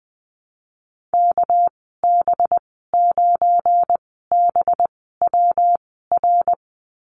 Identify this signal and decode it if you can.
Morse code, heard as a keyed tone, about 20 wpm, spaced as for 15 wpm, 700 Hz. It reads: KB9BWR
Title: 200 Lesson 32g - Letter k - Call Signs 20wpm